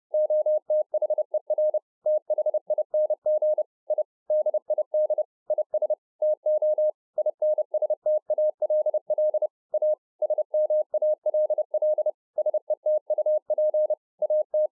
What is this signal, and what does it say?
Morse code, heard as a keyed tone, about 30 wpm, 610 Hz. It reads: OTHER THING I DID IS TO INSTALL A SMALL SETUP AT